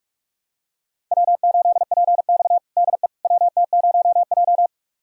Morse code, heard as a keyed tone, about 41 wpm, 700 Hz. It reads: W8PX DE WT0J
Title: USA Callsign DE Callsign - 4 Characters - 45wpm